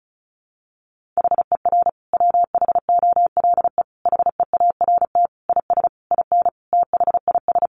Morse code, heard as a keyed tone, about 34 wpm, 700 Hz. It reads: HER WHOLE HEART IS IN THIS
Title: Sentences from Top 1500 Words 35wpm